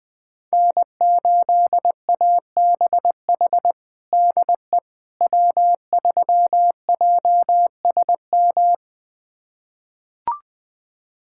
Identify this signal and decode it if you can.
Morse code, heard as a keyed tone, about 21 wpm, 700 Hz. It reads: N8ABH DE W3JSM
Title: USA Callsign DE Callsign - 5 Characters - 20wpm